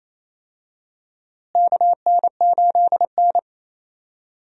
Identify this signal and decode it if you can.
Morse code, heard as a keyed tone, about 28 wpm, 700 Hz. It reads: KN8N
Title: USA Call Signs - 4 Characters - Repeated 2x - 28wpm